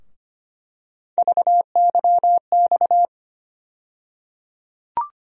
Morse code, heard as a keyed tone, about 25 wpm, 700 Hz. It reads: VYX
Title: USA Call Sign Postfixes - 3 Letters 25wpm